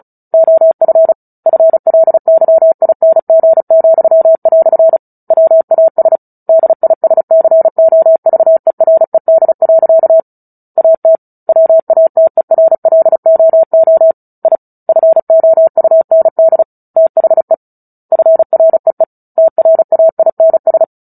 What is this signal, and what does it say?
Morse code, heard as a keyed tone, about 34 wpm, 650 Hz. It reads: OF FLYING," WAS DISCOVERED. AT WATERLOO I FOUND THE FREE TRAINS
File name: Book Two / 09-Chapter 9 - 35 WPM